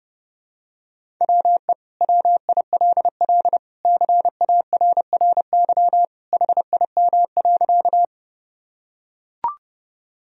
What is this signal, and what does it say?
Morse code, heard as a keyed tone, about 30 wpm, 700 Hz. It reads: WE WILL CARRY HIM.
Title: Sentences from Top 1000 Words - Review Part 1 - 30wpm